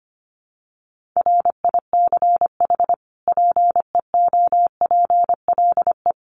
Morse code, heard as a keyed tone, about 25 wpm, 700 Hz. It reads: RICH PEOPLE
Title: Sentences from Top 1100 Words 25wpm